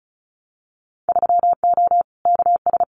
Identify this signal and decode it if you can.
Morse code, heard as a keyed tone, about 34 wpm, 700 Hz. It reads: 3O KS
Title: Winter Field Day Exchanges 35wpm